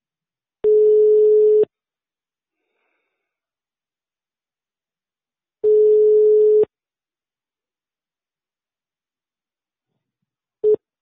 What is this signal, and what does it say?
Ringback tone (Europe, 425 Hz).